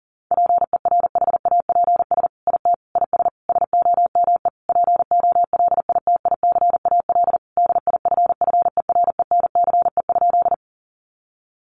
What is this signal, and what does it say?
Morse code, heard as a keyed tone, about 40 wpm, 700 Hz. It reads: PERHAPS IT IS SOME POLITICAL DIFFERENCE?
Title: Sentences from Top 800 Words 40wpm